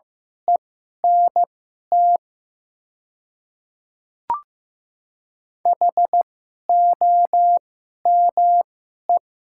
Morse code, heard as a keyed tone, about 15 wpm, spaced as for 11 wpm, 700 Hz. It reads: ENT HOME
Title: 2x Character Spacing - Top 200 Words - 15wpm